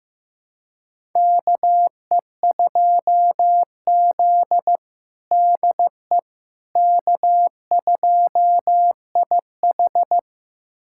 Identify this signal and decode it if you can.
Morse code, heard as a keyed tone, about 15 wpm, 700 Hz. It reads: KE2Z DE K2IH
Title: USA Callsign DE Callsign - 4 Characters - 15wpm